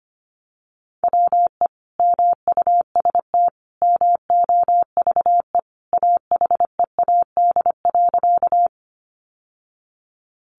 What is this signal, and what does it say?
Morse code, heard as a keyed tone, about 25 wpm, 700 Hz. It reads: WE MUST MOVE AHEAD.
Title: Sentences from Top 1100 Words 25wpm